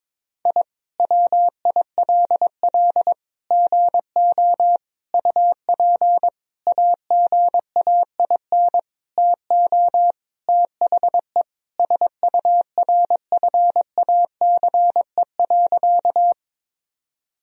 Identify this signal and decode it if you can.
Morse code, heard as a keyed tone, about 22 wpm, 700 Hz. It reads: I WILL GO UP AGAIN TO THE SURFACE.
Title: Sentences from Top 1000 Words - Review Part 3 - 22wpm